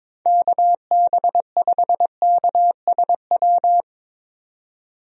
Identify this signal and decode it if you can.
Morse code, heard as a keyed tone, about 22 wpm, 700 Hz. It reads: KB5KSW
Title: USA Call Signs - 6 Characters - 22wpm